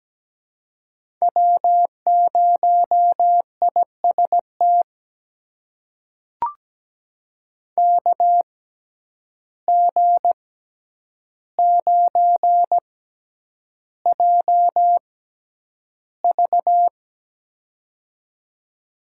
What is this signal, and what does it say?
Morse code, heard as a keyed tone, about 17 wpm, spaced as for 9 wpm, 700 Hz. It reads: W0IST KG9JV
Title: USA Call Signs - 5 Characters - 6x Character Spacing - 17wpm